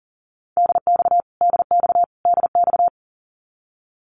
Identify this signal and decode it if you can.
Morse code, heard as a keyed tone, about 40 wpm, 700 Hz. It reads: DX DX DX
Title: Abbreviations Repeated 3x - 40wpm